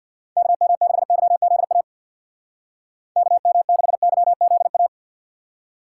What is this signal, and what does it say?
Morse code, heard as a keyed tone, about 59 wpm, 700 Hz. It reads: KM6YZA KM6YZA
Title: Speed Racing - USA Call Signs 50wpm